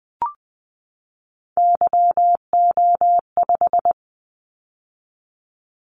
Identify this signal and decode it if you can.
Morse code, heard as a keyed tone, about 20 wpm, 700 Hz. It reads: YO5
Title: Callsign Prefixes for Top 25 DXCC - 20wpm